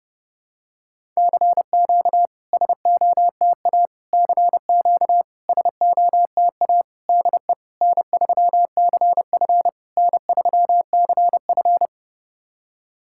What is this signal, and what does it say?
Morse code, heard as a keyed tone, about 30 wpm, 700 Hz. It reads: CQ SOTA CQ SOTA DE N3CF N3CF
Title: SOTA QSO Practice 30wpm